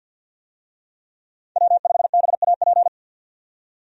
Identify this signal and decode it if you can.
Morse code, heard as a keyed tone, about 50 wpm, 700 Hz. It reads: W5BAP
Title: USA Call Signs - 5 Characters - 50wpm